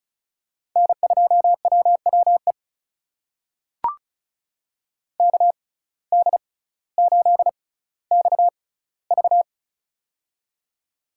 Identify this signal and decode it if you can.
Morse code, heard as a keyed tone, about 36 wpm, spaced as for 18 wpm, 700 Hz. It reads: N2WWE KD8XV
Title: USA Call Signs - 5 Characters - 6x Character Spacing - 35wpm